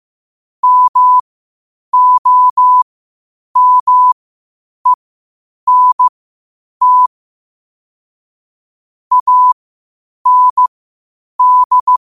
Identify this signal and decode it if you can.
Morse code, heard as a keyed tone, about 15 wpm, spaced as for 8 wpm, 1000 Hz. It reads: MOMENT AND
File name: qotdfiles / 8wpm